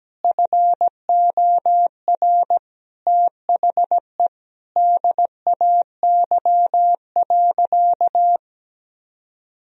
Morse code, heard as a keyed tone, about 17 wpm, 700 Hz. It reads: FOR THE DAY.
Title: Sentences from Top 1000 Words - Review Part 4 - 17wpm